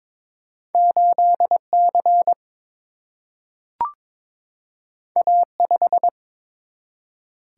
Morse code, heard as a keyed tone, about 22 wpm, 700 Hz. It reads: 8 C A 5
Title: Single-Letter Single-Number 22wpm